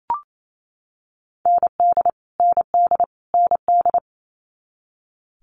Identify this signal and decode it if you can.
Morse code, heard as a keyed tone, about 28 wpm, 700 Hz. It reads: ND ND ND
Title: Multi-letter Phonemes Repeated 3x 28wpm